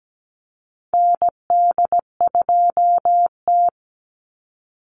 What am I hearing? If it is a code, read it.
Morse code, heard as a keyed tone, about 17 wpm, 700 Hz. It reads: ND2T
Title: Call Signs Repeated 2x - 17wpm